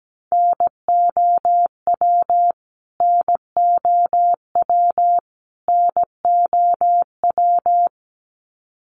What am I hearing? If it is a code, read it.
Morse code, heard as a keyed tone, about 17 wpm, 700 Hz. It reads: NOW NOW NOW
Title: Top 300 Words Repeated 17wpm